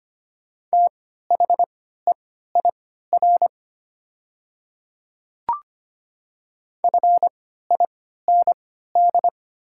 Morse code, heard as a keyed tone, about 25 wpm, spaced as for 14 wpm, 700 Hz. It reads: THEIR FIND
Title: 3x Character Spacing - Top 100 Words - 25wpm